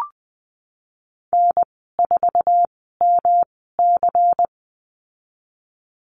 Morse code, heard as a keyed tone, about 19 wpm, spaced as for 14 wpm, 700 Hz. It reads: N4MC
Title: 128 Lesson 21g - Number 6 - Call Signs 20wpm